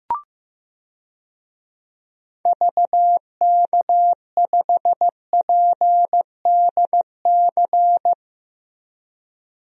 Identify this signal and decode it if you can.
Morse code, heard as a keyed tone, about 16 wpm, 700 Hz. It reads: VK5PDC
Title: DXCC-100 Call Signs - 6 Characters - Repeated 2x - 15wpm